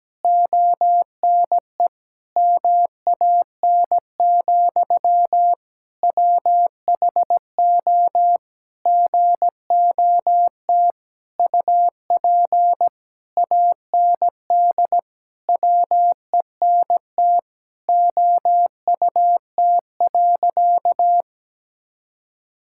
Morse code, heard as a keyed tone, about 17 wpm, 700 Hz. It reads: ONE MAN, WHO GOT UP AND WENT OUT.
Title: Sentences from Top 1600 Words 17wpm